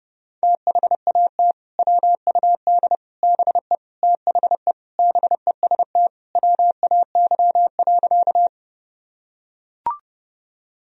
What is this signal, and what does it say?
Morse code, heard as a keyed tone, about 30 wpm, 700 Hz. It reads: THAT WUD BE THE BEST WAY.
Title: Sentences from Top 400 Words and Abbreviations 30wpm